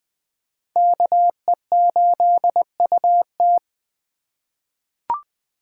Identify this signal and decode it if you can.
Morse code, heard as a keyed tone, about 20 wpm, 700 Hz. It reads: KE8UT
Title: USA Call Signs - 4x Character Spacing - 20wpm